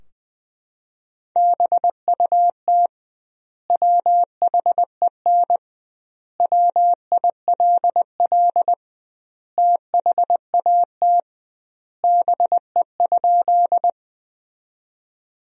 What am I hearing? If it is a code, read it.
Morse code, heard as a keyed tone, about 20 wpm, 700 Hz. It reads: BUT WHEN WILL THAT BE?
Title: 2x Word-Spacing - Sentences from Top 100 Words - 20wpm